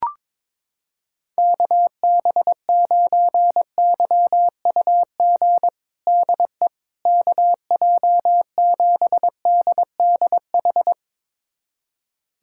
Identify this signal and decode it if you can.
Morse code, heard as a keyed tone, about 22 wpm, 700 Hz. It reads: KB9YUG DE KJ7DDH
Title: USA Callsign DE Callsign - 6 Characters - 22wpm